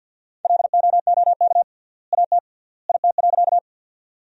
Morse code, heard as a keyed tone, about 52 wpm, 700 Hz. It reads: LOOK AT IT.
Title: 3x Word-Spacing - Sentences from Top 100 Words - 50wpm